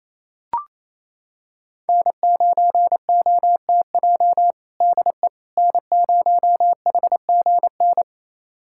Morse code, heard as a keyed tone, about 28 wpm, 700 Hz. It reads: N9OTJ DE N0HGN
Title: USA Callsign DE Callsign - 5 Characters - 28wpm